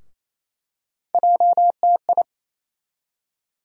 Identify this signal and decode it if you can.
Morse code, heard as a keyed tone, about 28 wpm, 700 Hz. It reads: JTI
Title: USA Call Sign Postfixes - 3 Letters 28wpm